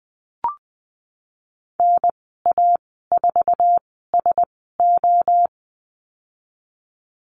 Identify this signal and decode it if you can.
Morse code, heard as a keyed tone, about 20 wpm, spaced as for 15 wpm, 700 Hz. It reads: NA4SO
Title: 048 Lesson 9g - Number 4 - Call Signs 20wpm